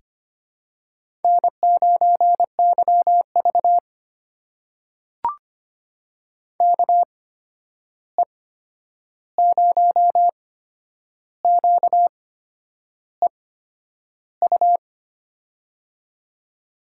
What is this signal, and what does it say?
Morse code, heard as a keyed tone, about 25 wpm, spaced as for 9 wpm, 700 Hz. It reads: N9YV KE0QEU
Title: USA Call Signs - 8x Character Spacing - 25wpm